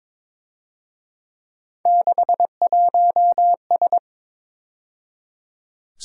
Morse code, heard as a keyed tone, about 26 wpm, 700 Hz. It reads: 61S
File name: Speed-Racing - Three Letter-Number Combinations - 22wpm